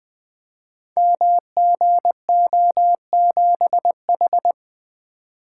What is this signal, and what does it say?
Morse code, heard as a keyed tone, about 20 wpm, 700 Hz. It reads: MGO7H
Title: Code Groups - 5 Letter:Numbers - 20wpm